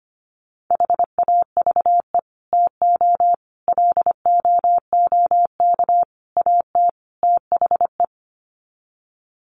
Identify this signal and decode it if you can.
Morse code, heard as a keyed tone, about 25 wpm, 700 Hz. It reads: HAVE TO LOOK AT THE
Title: Sets of 5 Words Encore 25wpm